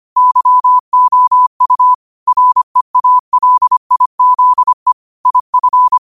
Morse code, heard as a keyed tone, about 25 wpm, 1000 Hz. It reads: YOU REALIZE IF